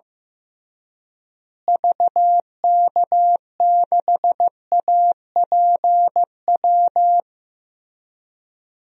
Morse code, heard as a keyed tone, about 15 wpm, 700 Hz. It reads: VK6APW
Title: DXCC-100 Call Signs - 6 Characters - Repeated 3x - 15wpm